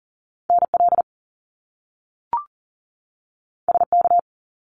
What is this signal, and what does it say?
Morse code, heard as a keyed tone, about 40 wpm, 700 Hz. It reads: NL SK
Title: Canadian Provinces and Territories 40wpm